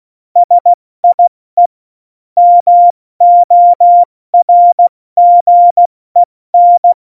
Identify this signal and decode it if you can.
Morse code, heard as a keyed tone, about 16 wpm, spaced as for 14 wpm, 700 Hz. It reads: SIE MORGEN
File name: Morsekurs / Witze / Witz 08. September 2020, 70 F80